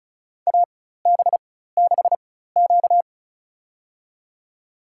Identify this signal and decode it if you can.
Morse code, heard as a keyed tone, about 36 wpm, spaced as for 16 wpm, 700 Hz. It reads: AB6Q
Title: USA Call Signs - 4x Character Spacing - 35wpm